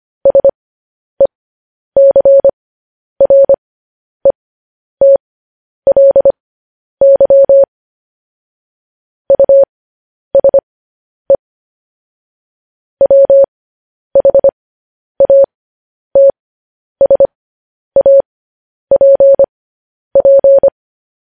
Morse code, heard as a keyed tone, about 25 wpm, spaced as for 10 wpm, 550 Hz. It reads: SECRETLY USE WHATSAPP
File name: podcast / News headlines-1x-25x10